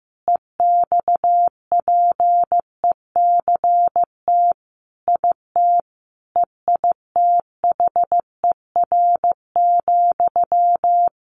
Morse code, heard as a keyed tone, about 15 wpm, 700 Hz. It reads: EXPECT IT EITHER,
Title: Sentences from Top 1100 Words 15wpm